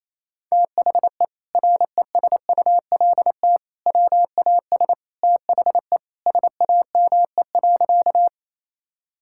Morse code, heard as a keyed tone, about 28 wpm, 700 Hz. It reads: THE RESULT WAS THE SAME.
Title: Sentences from Top 1000 Words - Review Part 5 - 28wpm